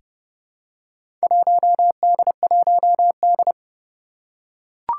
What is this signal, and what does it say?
Morse code, heard as a keyed tone, about 30 wpm, 700 Hz. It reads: 1D1D
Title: Four Challenging Letter-Number Combinations 30wpm